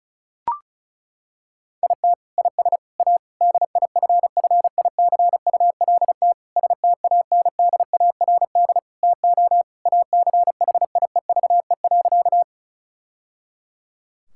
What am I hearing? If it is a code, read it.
Morse code, heard as a keyed tone, about 34 wpm, 700 Hz. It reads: IT IS A DIFFICULT STANDARD TO ACHIEVE.
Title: Sentences from Top 1000 Words - Review Part 6 - 35wpm